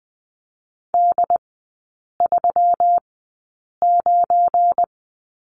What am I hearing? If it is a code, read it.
Morse code, heard as a keyed tone, about 20 wpm, spaced as for 8 wpm, 700 Hz. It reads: D39
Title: 184 Lesson 30e - Letter b - Three Characters 20wpm